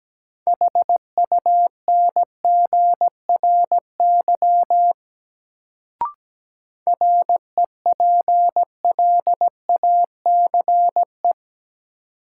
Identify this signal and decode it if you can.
Morse code, heard as a keyed tone, about 17 wpm, 700 Hz. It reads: HUNGRY REPLACE
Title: Sentences from Top 1400 Words - New Word Drill - 17wpm